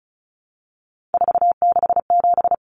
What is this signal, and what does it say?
Morse code, heard as a keyed tone, about 34 wpm, 700 Hz. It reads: 467
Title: Common Cut Numbers - 3-Digit 35wpm